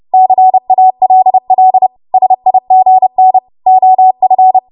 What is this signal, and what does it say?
Morse code, heard as a keyed tone, about 30 wpm, 750 Hz. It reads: CALL SIGN OF
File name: Archive / 30 WPM / 30WPM